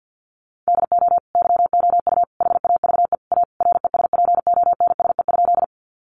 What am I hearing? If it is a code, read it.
Morse code, heard as a keyed tone, about 50 wpm, 700 Hz. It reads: DO YOU HAVE A RESPONSE?
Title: Sentences from Top 1400 Words 50wpm